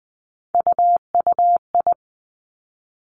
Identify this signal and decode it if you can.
Morse code, heard as a keyed tone, about 19 wpm, 700 Hz. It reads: UUI
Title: Three Challenging Letter-Number Combinations 20wpm